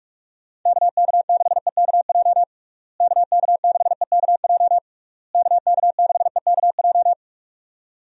Morse code, heard as a keyed tone, about 46 wpm, 700 Hz. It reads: KK6EKJ KK6EKJ KK6EKJ
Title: USA Call Signs - 6 Characters - Repeated 3x - 45wpm